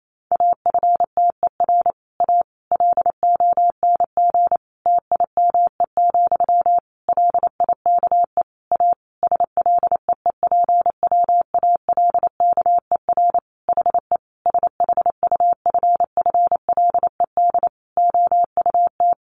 Morse code, heard as a keyed tone, about 28 wpm, 700 Hz. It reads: AFTER A LONG TIME, LIKE A SLEEPWALKER HE SHUFFLED OUT
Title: A Sound Of Thunder 28wpm